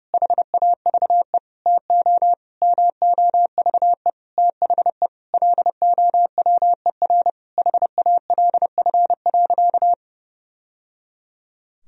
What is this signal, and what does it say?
Morse code, heard as a keyed tone, about 30 wpm, 700 Hz. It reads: HAVE TO MOVE THE LOWER HALF.
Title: Sentences from Top 1000 Words - Review Part 4 - 30wpm